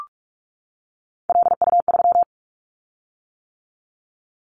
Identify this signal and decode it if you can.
Morse code, heard as a keyed tone, about 44 wpm, 700 Hz. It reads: LU3
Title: Callsign Prefixes for Top 100 DXCC - 45wpm